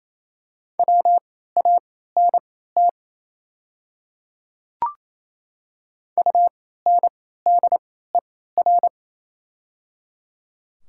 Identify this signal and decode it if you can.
Morse code, heard as a keyed tone, about 27 wpm, spaced as for 16 wpm, 700 Hz. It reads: WANT UNDER
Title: 3x Character Spacing - Top 200 Words - 28wpm